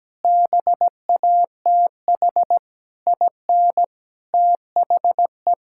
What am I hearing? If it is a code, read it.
Morse code, heard as a keyed tone, about 17 wpm, 700 Hz. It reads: BATH IN THE
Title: Sentences from Top 1600 Words 17wpm